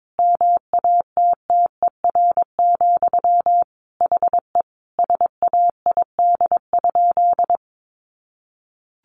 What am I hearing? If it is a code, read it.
Morse code, heard as a keyed tone, about 22 wpm, 700 Hz. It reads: MATTER, HE SAID?
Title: Sentences from Top 700 Words 22wpm